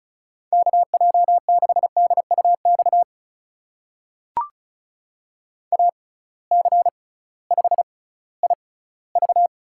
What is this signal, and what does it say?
Morse code, heard as a keyed tone, about 36 wpm, spaced as for 19 wpm, 700 Hz. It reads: KJ6DUX AC5IV
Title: USA Call Signs - 6x Character Spacing - 35wpm